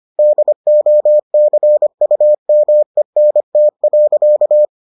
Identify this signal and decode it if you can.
Morse code, heard as a keyed tone, about 25 wpm, 600 Hz. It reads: DOCUMENT.